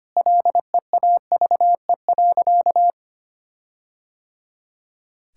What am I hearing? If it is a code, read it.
Morse code, heard as a keyed tone, about 25 wpm, 700 Hz. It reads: LEAVE.
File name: practice / 25wpm / Sentences with 4-Word Phrases - 25wpm